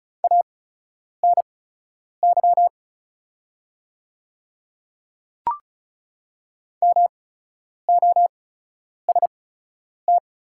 Morse code, heard as a keyed tone, about 36 wpm, spaced as for 10 wpm, 700 Hz. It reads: ANY MOST
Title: 8x Character Spacing - Top 200 Words - 35wpm